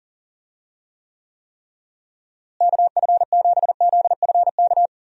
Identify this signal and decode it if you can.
Morse code, heard as a keyed tone, about 40 wpm, 700 Hz. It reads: KF7ZFK
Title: USA Call Signs - 6 Characters - Repeated 3x - 40wpm